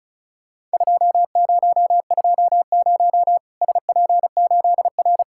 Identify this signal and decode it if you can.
Morse code, heard as a keyed tone, about 36 wpm, 700 Hz. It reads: 2020 SP8R
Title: Worked All Europe DX Contest - QTC Exchange - 35wpm